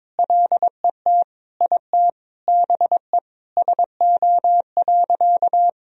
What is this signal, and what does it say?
Morse code, heard as a keyed tone, about 22 wpm, 700 Hz. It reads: LET IT BE SO.
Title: Sentences from Top 1000 Words - Review Part 1 - 22wpm